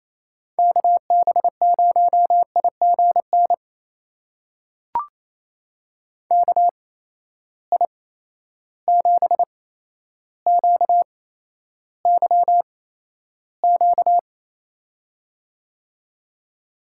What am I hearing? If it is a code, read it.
Morse code, heard as a keyed tone, about 28 wpm, spaced as for 12 wpm, 700 Hz. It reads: KB0IGN KI7QYQ
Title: USA Call Signs - 6 Characters - 8x Character Spacing - 28wpm